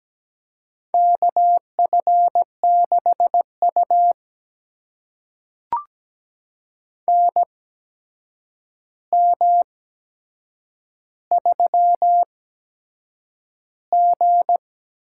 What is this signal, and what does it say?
Morse code, heard as a keyed tone, about 17 wpm, spaced as for 7 wpm, 700 Hz. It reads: KF6U NM3G
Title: USA Call Signs - 4 Characters - 8x Character Spacing - 17wpm